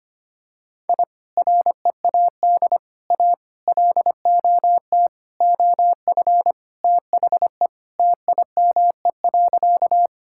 Morse code, heard as a keyed tone, about 25 wpm, 700 Hz. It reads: I READ A LOT OF THE TIME.
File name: practice / 25wpm / Sentences with 4-Word Phrases - 25wpm